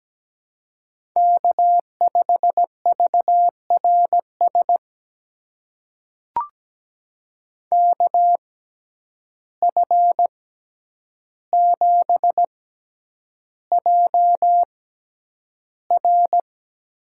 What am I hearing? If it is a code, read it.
Morse code, heard as a keyed tone, about 17 wpm, spaced as for 9 wpm, 700 Hz. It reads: K5VRS KF7JR
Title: USA Call Signs - 5 Characters - 6x Character Spacing - 17wpm